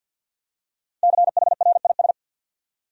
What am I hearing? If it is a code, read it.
Morse code, heard as a keyed tone, about 51 wpm, 700 Hz. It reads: CHRIS
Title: USA Common Names - Max 5 Letters - 50wpm